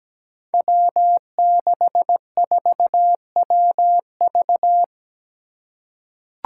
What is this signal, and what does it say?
Morse code, heard as a keyed tone, about 17 wpm, 700 Hz. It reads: W64WV
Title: Five Challenging Letter-Number Combinations 17wpm